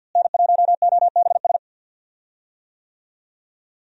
Morse code, heard as a keyed tone, about 52 wpm, 700 Hz. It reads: N1OBS
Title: USA Call Signs - 5 Characters - Repeated 3x - 50wpm